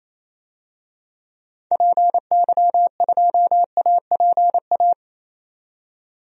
Morse code, heard as a keyed tone, about 28 wpm, 700 Hz. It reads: PY2APA
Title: DXCC-100 Call Signs - 6 Characters - 28wpm